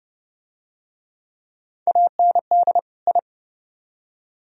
Morse code, heard as a keyed tone, about 30 wpm, 700 Hz. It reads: AND I